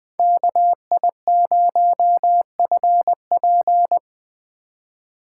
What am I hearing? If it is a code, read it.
Morse code, heard as a keyed tone, about 20 wpm, 700 Hz. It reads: KI0FP
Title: USA Call Signs - 5 Characters - 3x Character Spacing - 20wpm